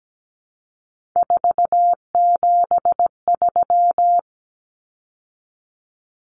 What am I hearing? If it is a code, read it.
Morse code, heard as a keyed tone, about 17 wpm, 700 Hz. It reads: 473
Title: Numbers - 3-Digit 17wpm